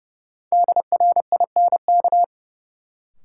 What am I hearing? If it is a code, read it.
Morse code, heard as a keyed tone, about 30 wpm, 700 Hz. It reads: DRINK
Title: Sentences from Top 1500 Words - New Word Drill Alt - 30wpm